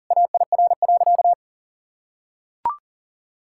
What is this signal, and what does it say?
Morse code, heard as a keyed tone, about 40 wpm, 700 Hz. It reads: AIR.
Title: Sentences from Top 1600 Words 40wpm